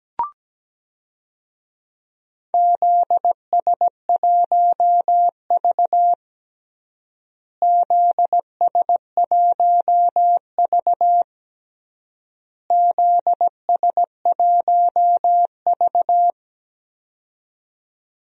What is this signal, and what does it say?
Morse code, heard as a keyed tone, about 17 wpm, 700 Hz. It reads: ZS1V ZS1V ZS1V
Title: DXCC-100 Call Signs - 4 Characters - Repeated 3x - 17wpm